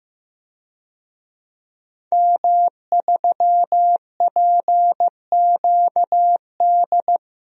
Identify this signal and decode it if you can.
Morse code, heard as a keyed tone, about 15 wpm, 700 Hz. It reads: M3PQD
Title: DXCC-100 Call Signs - 5 Characters - Repeated 3x - 15wpm